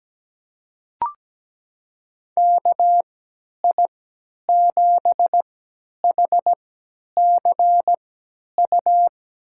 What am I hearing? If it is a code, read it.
Morse code, heard as a keyed tone, about 17 wpm, spaced as for 10 wpm, 700 Hz. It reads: KI7HCU
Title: USA Call Signs - 6 Characters - 3x Character Spacing - 17wpm